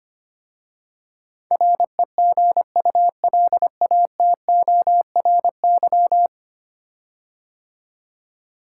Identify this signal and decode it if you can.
Morse code, heard as a keyed tone, about 25 wpm, 700 Hz. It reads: REGULATORY
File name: 10 Letter Words - First 3 Letters Given - 25wpm